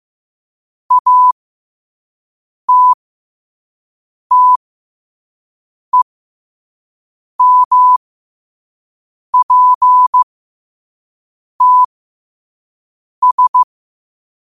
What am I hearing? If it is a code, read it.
Morse code, heard as a keyed tone, about 15 wpm, spaced as for 5 wpm, 1000 Hz. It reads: ATTEMPTS
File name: qotdfiles / 5wpm